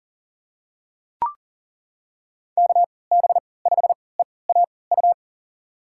Morse code, heard as a keyed tone, about 40 wpm, spaced as for 23 wpm, 700 Hz. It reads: KB5EAU
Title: USA Call Signs - 3x Character Spacing - 40wpm